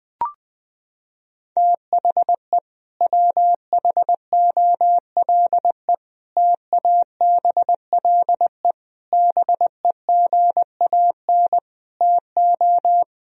Morse code, heard as a keyed tone, about 20 wpm, 700 Hz. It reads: THE WHOLE TABLE BEGAN TO
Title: Sentences from Top 1000 Words - Review Part 2 - 20wpm